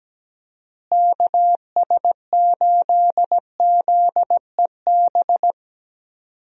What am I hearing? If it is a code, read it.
Morse code, heard as a keyed tone, about 17 wpm, 700 Hz. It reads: KS8ZEB
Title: USA Call Signs - 6 Characters - Repeated 3x - 17wpm